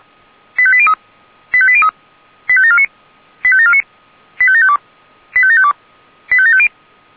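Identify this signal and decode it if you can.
Five-tone selective call (ZVEI), heard as quick runs of five tones, tones of about 70 ms. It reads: 85803 85803 85749 85749 85752 85752 85750
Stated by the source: ZVEI five-tone call 85827 received on an SDR.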